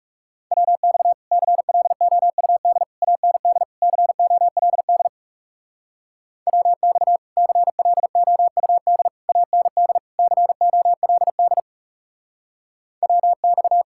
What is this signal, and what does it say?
Morse code, heard as a keyed tone, about 39 wpm, 700 Hz. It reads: WX CLOUD AND COLD WX CLOUD AND COLD WX